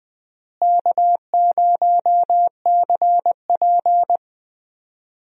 Morse code, heard as a keyed tone, about 20 wpm, 700 Hz. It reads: K0CP
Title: USA Call Signs - 4 Characters - Repeated 2x - 20wpm